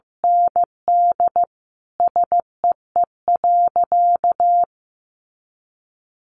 Morse code, heard as a keyed tone, about 15 wpm, 700 Hz. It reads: ND SEE.
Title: Sentences from Top 500 Words 15wpm